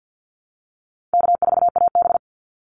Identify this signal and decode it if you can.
Morse code, heard as a keyed tone, about 50 wpm, 700 Hz. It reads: K4AB
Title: Easy Call Signs Repeated 3x - 50wpm